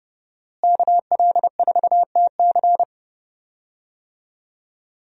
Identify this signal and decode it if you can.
Morse code, heard as a keyed tone, about 30 wpm, 700 Hz. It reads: KL4TC
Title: USA Call Signs - 5 Characters - Repeated 3x - 30wpm